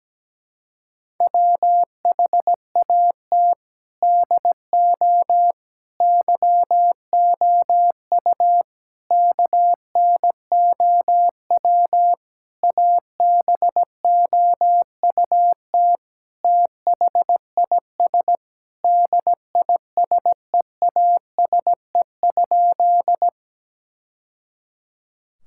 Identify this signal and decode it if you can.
Morse code, heard as a keyed tone, about 17 wpm, 700 Hz. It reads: WHAT DO YOU KNOW ABOUT THIS DISEASE?
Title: Sentences from Top 1000 Words - Review Part 6 - 17wpm